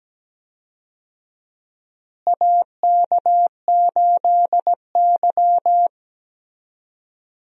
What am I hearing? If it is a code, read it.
Morse code, heard as a keyed tone, about 17 wpm, 700 Hz. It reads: AK8Y
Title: USA Call Signs - 4 Characters - Repeated 2x - 17wpm